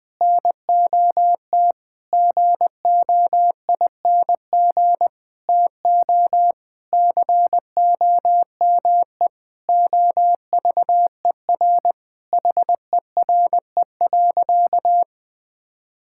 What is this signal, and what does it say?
Morse code, heard as a keyed tone, about 20 wpm, 700 Hz. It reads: NOT GOING TO COME OVER HERE.
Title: Sentences with 4-Word Phrases - 20wpm